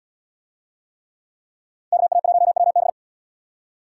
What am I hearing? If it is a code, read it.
Morse code, heard as a keyed tone, about 77 wpm, 700 Hz. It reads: KA1WB